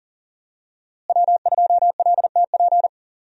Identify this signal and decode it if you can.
Morse code, heard as a keyed tone, about 40 wpm, 700 Hz. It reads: W2LTP
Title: USA Call Signs - 5 Characters - Repeated 3x - 40wpm